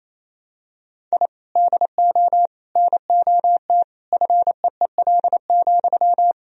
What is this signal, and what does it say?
Morse code, heard as a keyed tone, about 28 wpm, 700 Hz. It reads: I DO NOT FEEL,
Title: Sentences from Top 1000 Words - Review Part 3 - 28wpm